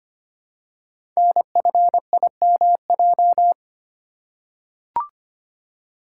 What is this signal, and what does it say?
Morse code, heard as a keyed tone, about 25 wpm, 700 Hz. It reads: NFIMJ
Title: Code Groups - 5 Letters - 25wpm